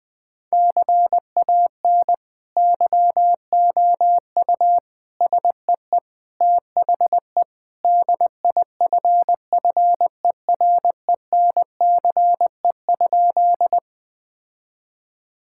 Morse code, heard as a keyed tone, about 20 wpm, 700 Hz. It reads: CAN YOU SEE THE DIFFERENCE?
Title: Sentences with 2-Word Phrases - 20wpm